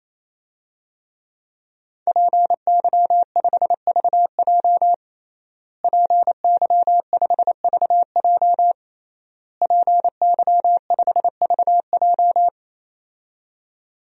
Morse code, heard as a keyed tone, about 28 wpm, 700 Hz. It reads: PY5VJ PY5VJ PY5VJ
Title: DXCC-100 Call Signs - 5 Characters - Repeated 3x - 28wpm